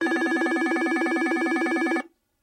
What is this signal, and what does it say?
There is no code.